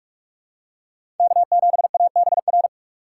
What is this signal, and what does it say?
Morse code, heard as a keyed tone, about 43 wpm, 700 Hz. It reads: K7ABR
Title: USA Call Signs - 5 Characters - 2x Character Spacing - 45wpm